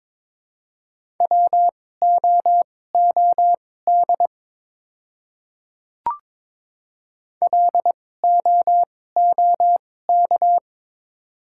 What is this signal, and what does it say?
Morse code, heard as a keyed tone, about 22 wpm, spaced as for 16 wpm, 700 Hz. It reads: WOOD LOOK
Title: 2x Character Spacing - Top 300 Words - 22wpm